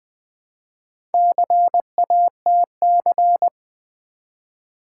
Morse code, heard as a keyed tone, about 20 wpm, 700 Hz. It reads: CATC
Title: Build A Word - 1 Letter At A Time - 3 to 5-Letter Words - 20wpm